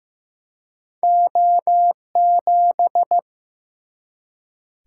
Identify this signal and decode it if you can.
Morse code, heard as a keyed tone, about 15 wpm, 700 Hz. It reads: O7
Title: Single-Letter Single-Number 15wpm